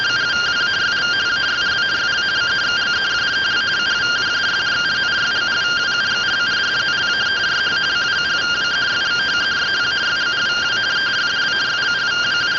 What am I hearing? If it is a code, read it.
Two-tone teleprinter signal, about 170 Hz shift.